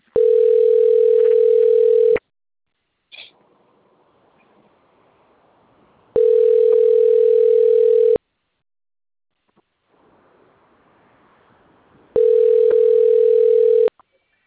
Ringback tone (North America), 440 + 480 Hz.